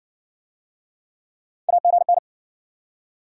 Morse code, heard as a keyed tone, about 60 wpm, 700 Hz. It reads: AGN